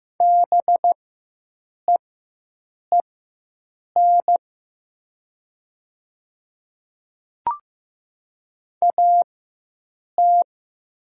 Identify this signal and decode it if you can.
Morse code, heard as a keyed tone, about 15 wpm, spaced as for 7 wpm, 700 Hz. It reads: BEEN AT
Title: 4x Character Spacing - Top 100 Words - 15wpm